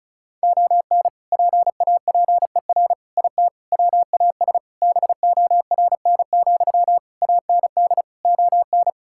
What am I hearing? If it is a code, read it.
Morse code, heard as a keyed tone, about 36 wpm, 700 Hz. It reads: ON PAPER IT WAS BORN, AND ON
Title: Sentences from Top 1600 Words 35wpm